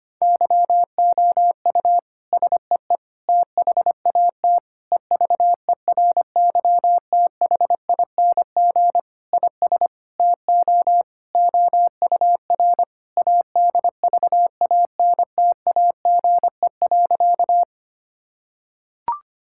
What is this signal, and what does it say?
Morse code, heard as a keyed tone, about 25 wpm, 700 Hz. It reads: YOU SEE THAT EVERYTHING IS TO OUR ADVANTAGE.
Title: Sentences from Top 1000 Words - Review Part 6 - 25wpm